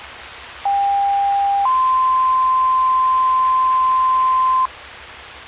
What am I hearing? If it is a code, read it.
Two-tone page, heard as a one-second tone, then a three-second tone, about 800 then 1040 Hz. It is heard from 0.7 to 4.7 s.